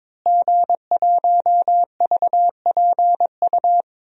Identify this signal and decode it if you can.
Morse code, heard as a keyed tone, about 21 wpm, 700 Hz. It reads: G1VPU
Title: DXCC-100 Call Signs - 5 Characters - Repeated 3x - 22wpm